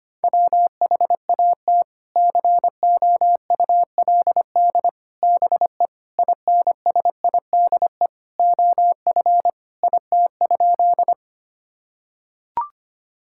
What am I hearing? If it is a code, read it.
Morse code, heard as a keyed tone, about 25 wpm, 700 Hz. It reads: WHAT COULD BE INSIDE OF IT?
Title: Sentences from Top 1000 Words - Review Part 6 - 25wpm